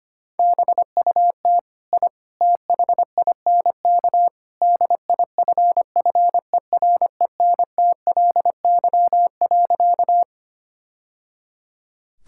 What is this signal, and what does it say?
Morse code, heard as a keyed tone, about 25 wpm, 700 Hz. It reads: BUT I THINK DIFFERENTLY.
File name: practice / 25wpm / Sentences with 3-Word Phrases - 25wpm